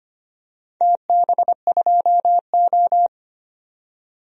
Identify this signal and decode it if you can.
Morse code, heard as a keyed tone, about 25 wpm, 700 Hz. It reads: TB2O
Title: DXCC-100 Call Signs - 4 Characters - 25wpm